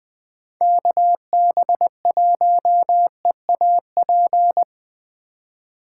Morse code, heard as a keyed tone, about 23 wpm, 700 Hz. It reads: KB1EAP
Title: Speed Racing - USA 2x3 Call Signs 20wpm